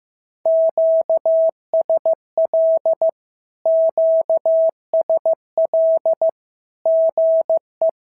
Morse code, heard as a keyed tone, about 15 wpm, 650 Hz. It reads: QSL QSL GE
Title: SOTA QSO Practice 15wpm